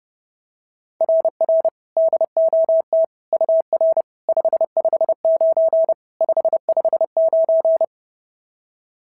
Morse code, heard as a keyed tone, about 26 wpm, 650 Hz. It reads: RR DOT UR 559 559
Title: SOTA QSO Practice 30wpm